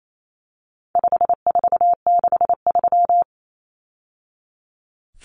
Morse code, heard as a keyed tone, about 28 wpm, 700 Hz. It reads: 5463
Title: Numbers - 4-Digit 28wpm